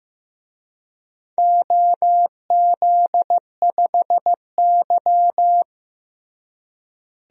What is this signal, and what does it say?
Morse code, heard as a keyed tone, about 16 wpm, 700 Hz. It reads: OZ5Y
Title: DXCC-100 Call Signs - 4 Characters - Repeated 2x - 15wpm